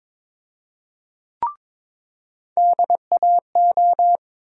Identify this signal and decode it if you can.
Morse code, heard as a keyed tone, about 22 wpm, 700 Hz. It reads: DAO
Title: USA Call Sign Postfixes - 3 Letters 22wpm